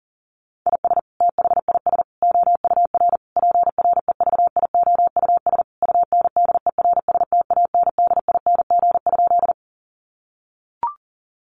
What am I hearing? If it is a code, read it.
Morse code, heard as a keyed tone, about 40 wpm, 700 Hz. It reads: IS THIS OUR PREVIOUS UNDERSTANDING?
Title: Sentences from Top 1500 Words 40wpm